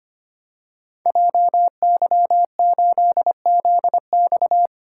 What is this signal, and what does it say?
Morse code, heard as a keyed tone, about 25 wpm, 700 Hz. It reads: JY8ZX
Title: Code Groups - 5 Letter:Numbers - 25wpm